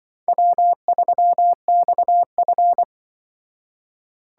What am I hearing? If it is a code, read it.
Morse code, heard as a keyed tone, about 25 wpm, 700 Hz. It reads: W3XF
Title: Speed Racing - USA 1x2 Call Signs 20wpm